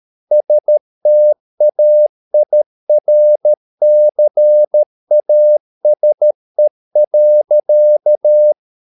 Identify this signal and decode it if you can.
Morse code, heard as a keyed tone, about 13 wpm, 600 Hz. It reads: STAIRCASE.